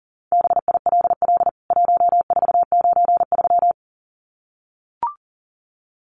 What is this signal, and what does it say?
Morse code, heard as a keyed tone, about 40 wpm, 700 Hz. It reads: BILL 1493
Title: CWT Exchange 40wpm